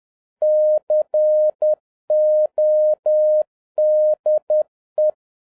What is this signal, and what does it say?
Morse code, heard as a keyed tone, about 10 wpm, 610 Hz. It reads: CODE